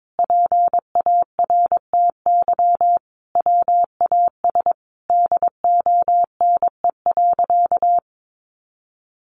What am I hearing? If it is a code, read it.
Morse code, heard as a keyed tone, about 22 wpm, 700 Hz. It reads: PARTY WAS DONE.
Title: Sentences from Top 1400 Words 22wpm